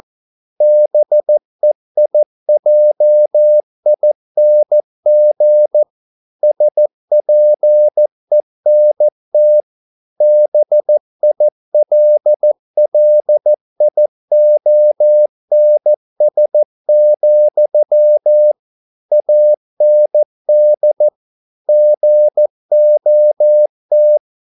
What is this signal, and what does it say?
Morse code, heard as a keyed tone, about 14 wpm, 600 Hz. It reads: BEIJING SPENT BILLIONS, AND GOT